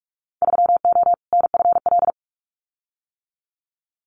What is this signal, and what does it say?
Morse code, heard as a keyed tone, about 44 wpm, 700 Hz. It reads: 3O NFL
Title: Winter Field Day Exchanges 45wpm